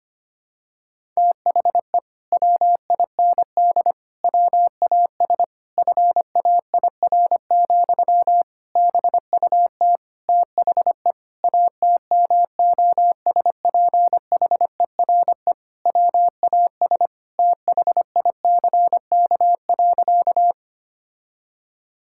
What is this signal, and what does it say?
Morse code, heard as a keyed tone, about 25 wpm, 700 Hz. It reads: THE WIND WAS FAIR, BUT THE ATMOSPHERE WAS THICK.
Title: Sentences from Top 1600 Words 25wpm